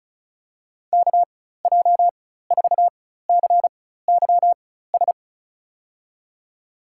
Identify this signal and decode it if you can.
Morse code, heard as a keyed tone, about 36 wpm, spaced as for 16 wpm, 700 Hz. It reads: KJ4CYS
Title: USA Call Signs - 4x Character Spacing - 35wpm